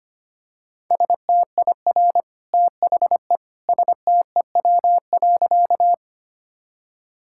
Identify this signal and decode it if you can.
Morse code, heard as a keyed tone, about 25 wpm, 700 Hz. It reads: STIR THE STEW.
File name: Easy Sentences with Hints - 25wpm